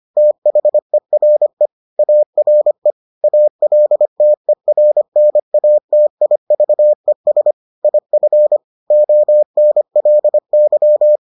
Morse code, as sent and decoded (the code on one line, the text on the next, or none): - .... .-. / .- .-. / .- .-.. - . .-. -. .- - .. ...- . ... / .. ..-. / --- -. .-.. -.--
THERE ARE ALTERNATIVES IF ONLY